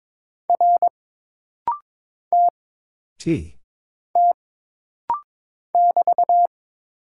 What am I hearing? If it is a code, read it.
Morse code, heard as a keyed tone, about 22 wpm, 700 Hz. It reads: R T T <BT>
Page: This rapid-fire timing is geared to push your ICR (Instant Character Recognition) skills.